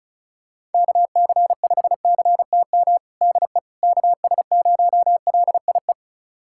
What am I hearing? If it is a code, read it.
Morse code, heard as a keyed tone, about 34 wpm, 700 Hz. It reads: KC5CTM DE KS0LIE
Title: USA Callsign DE Callsign - 6 Characters - 35wpm